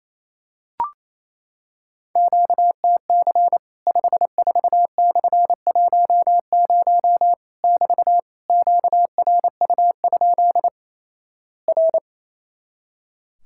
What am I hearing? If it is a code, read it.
Morse code, heard as a keyed tone, about 28 wpm, 700 Hz. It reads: QTC 54/10 <BT> QRU? R
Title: Worked All Europe DX Contest - QTC Exchange - 28wpm